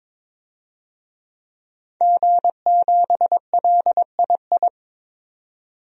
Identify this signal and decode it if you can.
Morse code, heard as a keyed tone, about 22 wpm, 700 Hz. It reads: G7LII
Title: DXCC-100 Call Signs - 5 Characters - 22wpm